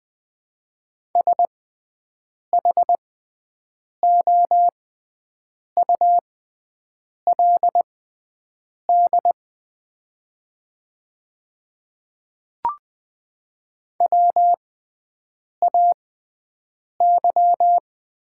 Morse code, heard as a keyed tone, about 20 wpm, spaced as for 7 wpm, 700 Hz. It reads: SHOULD WAY
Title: 6x Character Spacing - Top 500 Words - 20wpm